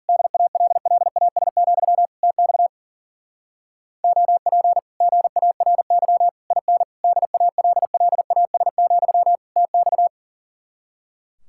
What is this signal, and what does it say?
Morse code, heard as a keyed tone, about 47 wpm, 700 Hz. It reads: DALLAS, TX OP GARY IN DALLAS, TX
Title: Speed-Racing Common QSO Phrases Encore - 40wpm